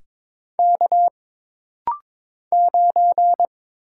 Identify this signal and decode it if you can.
Morse code, heard as a keyed tone, about 22 wpm, 700 Hz. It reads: K 9
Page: This rapid-fire timing is geared to push your ICR (Instant Character Recognition) skills.